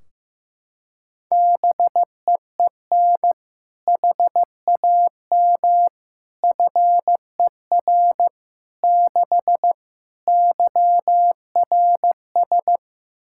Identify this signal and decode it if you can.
Morse code, heard as a keyed tone, about 15 wpm, 700 Hz. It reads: BEEN HAM FER 6 YRS
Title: Practice QSO - Exchange 3 - More Info - 15wpm